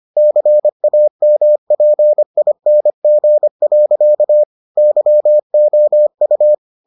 Morse code, heard as a keyed tone, about 25 wpm, 600 Hz. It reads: CAMPING. YOU